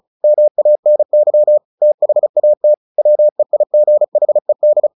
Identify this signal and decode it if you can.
Morse code, heard as a keyed tone, about 34 wpm, 600 Hz. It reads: MANY THAT WEIGHED
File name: the old man and the sea / the old man and the sea 1 35wpm